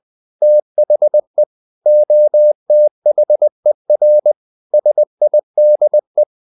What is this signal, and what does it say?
Morse code, heard as a keyed tone, about 20 wpm, 600 Hz. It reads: THE OTHER SIDE
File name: the old man and the sea part 2 20wpm